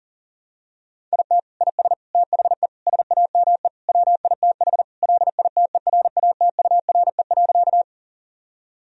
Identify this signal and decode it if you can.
Morse code, heard as a keyed tone, about 40 wpm, 700 Hz. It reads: IT IS THE SAME WITH LITERATURE.
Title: Sentences from Top 1000 Words - Review Part 3 - 40wpm